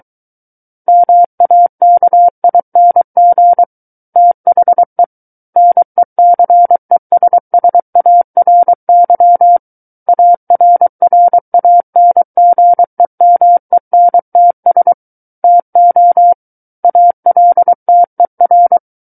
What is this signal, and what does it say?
Morse code, heard as a keyed tone, about 23 wpm, 700 Hz. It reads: MAKING THE NECESSARY ARRANGEMENTS TO ALTER
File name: Book One / 14-Chapter 14 - 23 WPM